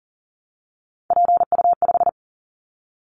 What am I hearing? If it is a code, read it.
Morse code, heard as a keyed tone, about 42 wpm, 700 Hz. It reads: PU5
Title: Callsign Prefixes for Top 100 DXCC - 40wpm